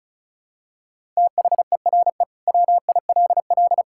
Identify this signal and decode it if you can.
Morse code, heard as a keyed tone, about 36 wpm, 700 Hz. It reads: THERE WILL
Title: Sentences from Top 1000 Words - Review Part 5 - 35wpm